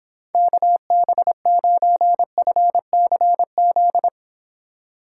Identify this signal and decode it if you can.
Morse code, heard as a keyed tone, about 26 wpm, 700 Hz. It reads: KB9FCZ